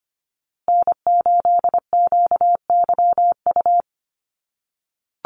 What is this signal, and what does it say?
Morse code, heard as a keyed tone, about 25 wpm, 700 Hz. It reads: N8QYU
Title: USA Call Signs Repeated 2x 25wpm